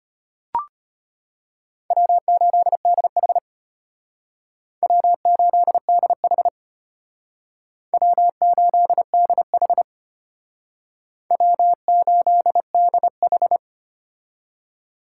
Morse code, heard as a keyed tone, about 33 wpm, 700 Hz. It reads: W8DH W8DH W8DH W8DH